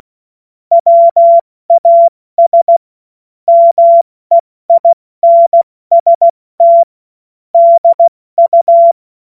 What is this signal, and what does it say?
Morse code, heard as a keyed tone, about 16 wpm, 680 Hz. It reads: WAS MEINST DU